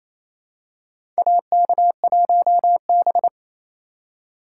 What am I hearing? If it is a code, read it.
Morse code, heard as a keyed tone, about 28 wpm, 700 Hz. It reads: AK1B
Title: USA Call Signs - 4 Characters - 2x Character Spacing - 28wpm